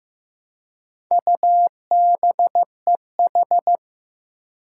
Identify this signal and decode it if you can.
Morse code, heard as a keyed tone, about 15 wpm, 700 Hz. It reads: UBEH
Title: Four Challenging Letter-Number Combinations 15wpm